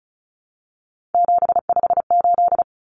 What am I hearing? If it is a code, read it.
Morse code, heard as a keyed tone, about 34 wpm, 700 Hz. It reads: 758
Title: Common Cut Numbers - 3-Digit 35wpm